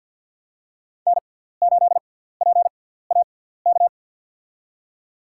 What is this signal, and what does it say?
Morse code, heard as a keyed tone, about 52 wpm, spaced as for 17 wpm, 700 Hz. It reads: N8PAK
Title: USA Call Signs - 5 Characters - 6x Character Spacing - 50wpm